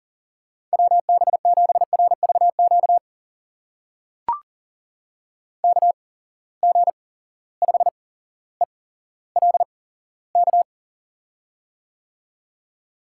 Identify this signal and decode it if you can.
Morse code, heard as a keyed tone, about 42 wpm, spaced as for 17 wpm, 700 Hz. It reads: WB7RVQ KG5ELK
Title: USA Call Signs - 6 Characters - 8x Character Spacing - 40wpm